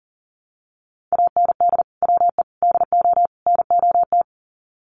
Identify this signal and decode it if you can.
Morse code, heard as a keyed tone, about 39 wpm, 700 Hz. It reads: AND WE DO NOT
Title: Sets of 4 Words 40wpm